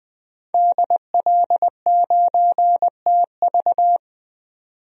Morse code, heard as a keyed tone, about 21 wpm, 700 Hz. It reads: DL9TV
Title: DXCC-100 Call Signs - 5 Characters - 20wpm